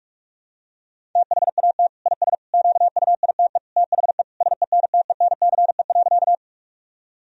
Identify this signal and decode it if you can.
Morse code, heard as a keyed tone, about 46 wpm, 700 Hz. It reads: THAT IS QUITE THE SENTENCE.
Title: Sentences from Top 1000 Words - Review Part 1 - 45wpm